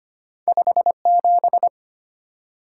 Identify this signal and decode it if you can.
Morse code, heard as a keyed tone, about 25 wpm, 690 Hz. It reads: 57
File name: Two Letter-Number Combinations - 25wpm